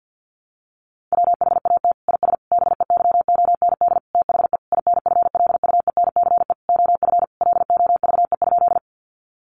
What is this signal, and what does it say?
Morse code, heard as a keyed tone, about 50 wpm, 700 Hz. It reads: WHAT IS BEYOND THE INFLUENCE OF LOVE?
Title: Sentences from Top 1100 Words 50wpm